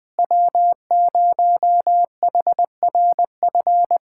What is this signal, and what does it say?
Morse code, heard as a keyed tone, about 20 wpm, 700 Hz. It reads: W0HRF
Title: USA Callsign DE Callsign - 5 Characters - 20wpm